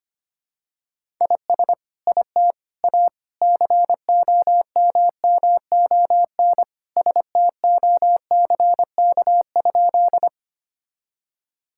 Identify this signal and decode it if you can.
Morse code, heard as a keyed tone, about 25 wpm, 700 Hz. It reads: IS IT A COMMON STOCK?
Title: Sentences from Top 1000 Words - Review Part 5 - 25wpm